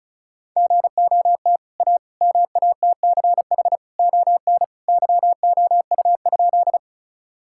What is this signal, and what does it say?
Morse code, heard as a keyed tone, about 34 wpm, 700 Hz. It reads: GOT A MATCH ON YOU?
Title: Sentences from Top 1600 Words 35wpm